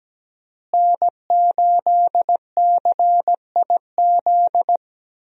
Morse code, heard as a keyed tone, about 17 wpm, 700 Hz. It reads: N8CIZ
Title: USA Call Signs - 5 Characters - Repeated 2x - 17wpm